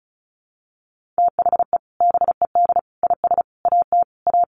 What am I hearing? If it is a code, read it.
Morse code, heard as a keyed tone, about 34 wpm, 700 Hz. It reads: THE BED IS AT A
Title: Sentences from Top 1500 Words 35wpm